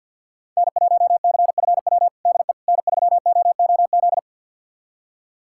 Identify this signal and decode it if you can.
Morse code, heard as a keyed tone, about 49 wpm, 700 Hz. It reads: N1CFW DE N3OOZ
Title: USA Callsign DE Callsign 50wpm